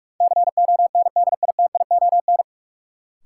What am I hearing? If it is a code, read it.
Morse code, heard as a keyed tone, about 46 wpm, 700 Hz. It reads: CONDITION
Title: Sentences from Top 1000 Words - New Word Drill - 45wpm